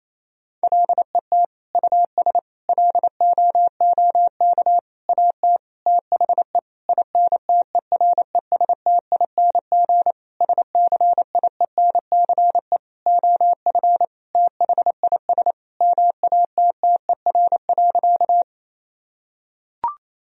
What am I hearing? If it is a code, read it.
Morse code, heard as a keyed tone, about 28 wpm, 700 Hz. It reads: LET US LOOK AT THE INTERESTING SCIENCE OF THIS MATTER.
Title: Sentences from Top 1000 Words - Review Part 2 - 28wpm